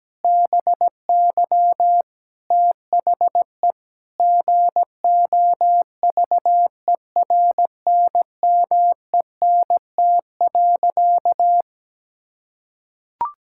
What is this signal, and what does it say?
Morse code, heard as a keyed tone, about 17 wpm, 700 Hz. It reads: BY THE GOVERNMENT.
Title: Sentences from Top 1000 Words - Review Part 6 - 17wpm